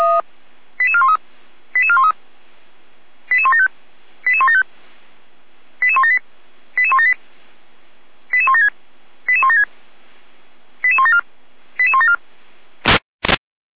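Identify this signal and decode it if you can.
Five-tone selective call (ZVEI), heard as quick runs of five tones, tones of about 70 ms. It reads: 88413 88413 88175 88175 88178 88178 88176 88176 88174 88174
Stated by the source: ZVEI five-tone selective call received by SDR.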